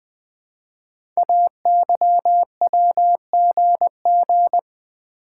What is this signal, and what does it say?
Morse code, heard as a keyed tone, about 20 wpm, 700 Hz. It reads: AYWGG
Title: Code Groups - 5 Letters - 20wpm